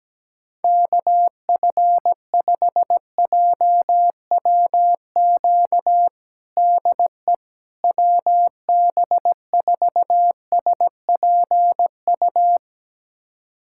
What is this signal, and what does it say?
Morse code, heard as a keyed tone, about 17 wpm, 700 Hz. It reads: KF5JWQ DE WB4SPU
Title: USA Callsign DE Callsign - 6 Characters - 17wpm